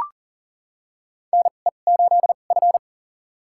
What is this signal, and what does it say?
Morse code, heard as a keyed tone, about 39 wpm, spaced as for 29 wpm, 700 Hz. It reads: NE8F
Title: USA Call Signs - 4 Characters - 2x Character Spacing - 40wpm